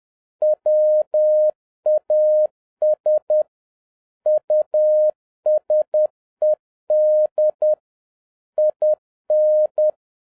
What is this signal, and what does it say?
Morse code, heard as a keyed tone, about 10 wpm, 610 Hz. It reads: WAS USED IN